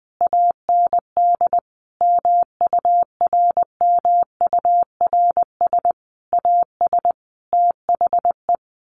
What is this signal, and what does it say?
Morse code, heard as a keyed tone, about 20 wpm, 700 Hz. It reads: AND MURMURS AS THE
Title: A Sound Of Thunder 20wpm